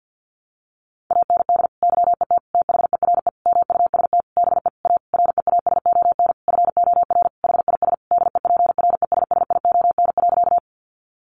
Morse code, heard as a keyed tone, about 50 wpm, 700 Hz. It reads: AND YET THERE MUST BE A REASON FOR HIS DEPRESSION.
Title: Sentences from Top 1300 Words 50wpm